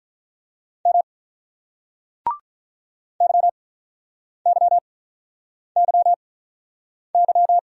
Morse code, heard as a keyed tone, about 48 wpm, spaced as for 9 wpm, 700 Hz. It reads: M YYYY